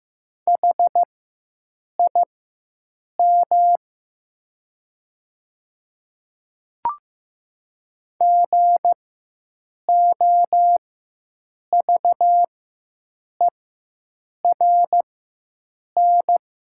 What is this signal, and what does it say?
Morse code, heard as a keyed tone, about 15 wpm, spaced as for 7 wpm, 700 Hz. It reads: HIM GOVERN
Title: 4x Character Spacing - Top 500 Words - 15wpm